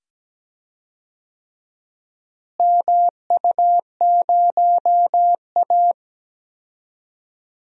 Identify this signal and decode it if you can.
Morse code, heard as a keyed tone, about 17 wpm, 700 Hz. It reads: MU0A
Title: DXCC-100 Call Signs - 4 Characters - Repeated 2x - 17wpm